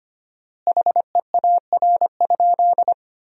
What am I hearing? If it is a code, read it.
Morse code, heard as a keyed tone, about 25 wpm, 700 Hz. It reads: HEAR?
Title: 4x Word-Spacing - Sentences from Top 300 Words - 25wpm